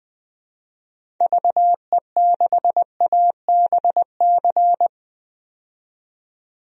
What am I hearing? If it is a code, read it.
Morse code, heard as a keyed tone, about 20 wpm, 700 Hz. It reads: VE6ABC
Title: DXCC-100 Call Signs - 6 Characters - 20wpm